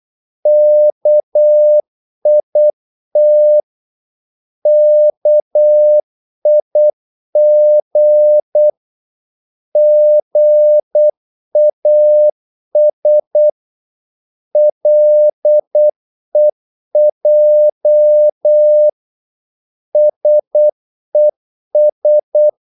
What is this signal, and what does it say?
Morse code, heard as a keyed tone, about 8 wpm, 600 Hz. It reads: KIT KIG GAS LEJ SES
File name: Korte ord DK 8wpm